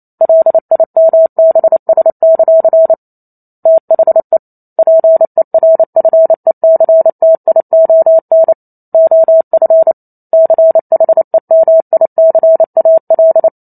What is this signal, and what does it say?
Morse code, heard as a keyed tone, about 28 wpm, 650 Hz. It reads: LIMBS; THE PERFECTION OF CHEMICAL